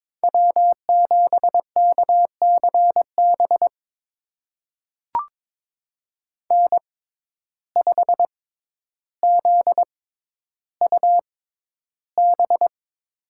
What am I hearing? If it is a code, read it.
Morse code, heard as a keyed tone, about 22 wpm, spaced as for 11 wpm, 700 Hz. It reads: W7KCB N5ZUB
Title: USA Call Signs - 5 Characters - 6x Character Spacing - 22wpm